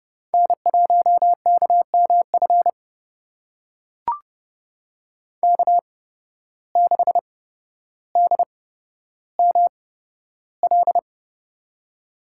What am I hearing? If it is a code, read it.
Morse code, heard as a keyed tone, about 30 wpm, spaced as for 13 wpm, 700 Hz. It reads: N1KMF K6DML
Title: USA Call Signs - 8x Character Spacing - 30wpm